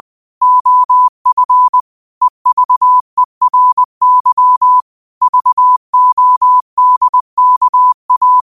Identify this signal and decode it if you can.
Morse code, heard as a keyed tone, about 20 wpm, 1000 Hz. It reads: OF EVERY VODKA
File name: qotdfiles / 20wpm